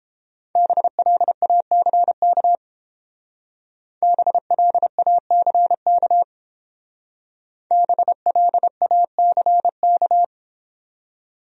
Morse code, heard as a keyed tone, about 31 wpm, 700 Hz. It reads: BLACK BLACK BLACK